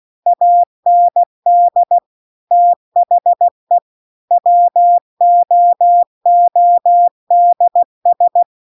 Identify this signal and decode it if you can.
Morse code, heard as a keyed tone, about 16 wpm, 700 Hz. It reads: AND THE WOODS
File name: CW-The-Big-Twoheaded-River-2-16WPM